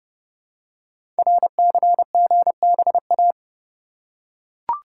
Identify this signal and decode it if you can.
Morse code, heard as a keyed tone, about 30 wpm, 700 Hz. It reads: RCGBA
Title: Code Groups - 5 Letters - 30wpm